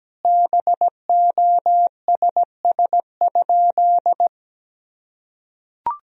Morse code, heard as a keyed tone, about 17 wpm, 700 Hz. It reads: BOSS?
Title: Sentences from Top 1000 Words - Review Part 4 - 17wpm